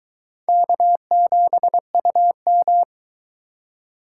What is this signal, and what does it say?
Morse code, heard as a keyed tone, about 22 wpm, 700 Hz. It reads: K7UM